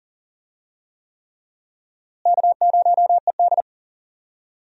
Morse code, heard as a keyed tone, about 40 wpm, 700 Hz. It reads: K0ED
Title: USA Call Signs - 4 Characters - Repeated 3x - 40wpm